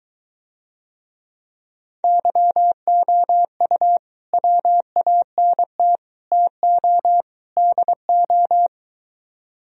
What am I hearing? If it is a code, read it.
Morse code, heard as a keyed tone, about 23 wpm, 700 Hz. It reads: YOU WANT TO DO